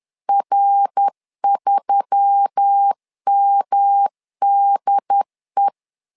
Morse code, heard as a keyed tone, about 10 wpm, 800 Hz. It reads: R3MDE
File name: LEZIONI / 4 Q1S3 10wpm 2427